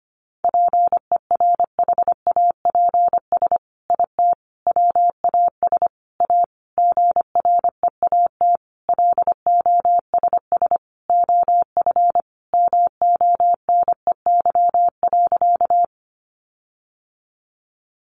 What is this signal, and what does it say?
Morse code, heard as a keyed tone, about 25 wpm, 700 Hz. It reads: PERHAPS IT WAS A GREAT LOSS OF MONEY.
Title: Sentences from Top 1400 Words 25wpm